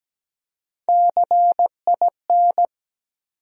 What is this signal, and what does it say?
Morse code, heard as a keyed tone, about 17 wpm, 700 Hz. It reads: CIN
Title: USA Call Sign Postfixes - 3 Letters 17wpm